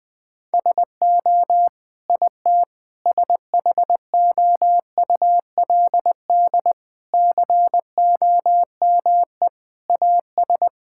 Morse code, heard as a keyed tone, about 19 wpm, 700 Hz. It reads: SO IT SHOULD COME AS
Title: Sentences with 9-Word Phrases - 20wpm